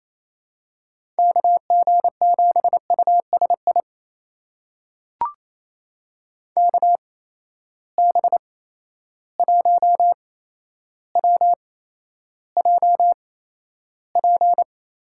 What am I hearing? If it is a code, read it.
Morse code, heard as a keyed tone, about 28 wpm, spaced as for 12 wpm, 700 Hz. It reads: KG7USI KB1WJP
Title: USA Call Signs - 6 Characters - 8x Character Spacing - 28wpm